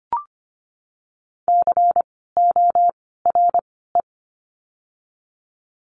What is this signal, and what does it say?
Morse code, heard as a keyed tone, about 25 wpm, 700 Hz. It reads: C O R E
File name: course / 25:10wpm / 094 Lesson 16f - Letter c - Words 25wpm